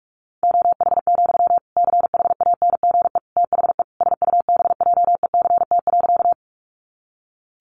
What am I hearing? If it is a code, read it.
Morse code, heard as a keyed tone, about 44 wpm, 700 Hz. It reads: OH, CHANGE THE SUBJECT.
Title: Sentences from Top 800 Words 45wpm